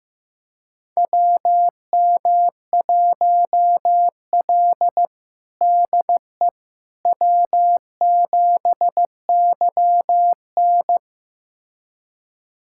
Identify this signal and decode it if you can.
Morse code, heard as a keyed tone, about 15 wpm, 700 Hz. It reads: WM1L DE W7YN
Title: USA Callsign DE Callsign - 4 Characters - 15wpm